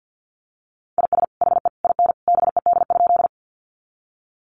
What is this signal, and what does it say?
Morse code, heard as a keyed tone, about 50 wpm, 700 Hz. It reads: IS HE IN BED?
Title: Sentences from Top 500 Words 50wpm